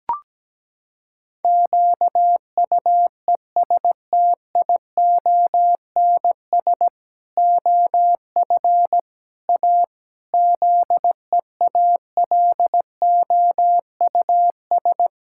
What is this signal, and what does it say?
Morse code, heard as a keyed tone, about 17 wpm, 700 Hz. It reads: QUESTIONS OF A ZEALOUS
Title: Pangram Challenge v2 - 17wpm